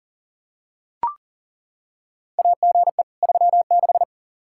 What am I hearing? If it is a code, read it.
Morse code, heard as a keyed tone, about 40 wpm, 700 Hz. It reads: AGE 36
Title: Practice QSO - Exchange 3 - More Info - 40wpm